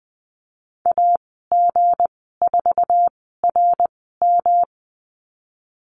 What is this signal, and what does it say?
Morse code, heard as a keyed tone, about 20 wpm, spaced as for 15 wpm, 700 Hz. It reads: AG4RM
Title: 167 Lesson 27g - Number 7 - Call Signs 20wpm